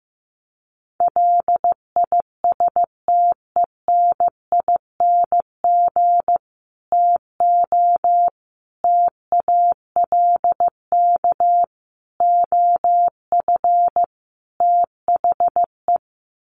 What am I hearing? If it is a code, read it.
Morse code, heard as a keyed tone, about 15 wpm, 700 Hz. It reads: LISTENING TO TALK OF THE
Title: Sets of 5 Words 15wpm